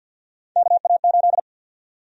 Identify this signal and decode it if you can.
Morse code, heard as a keyed tone, about 53 wpm, 700 Hz. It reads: KA8
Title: USA Call Sign Prefixes - Repeated 2x - 50wpm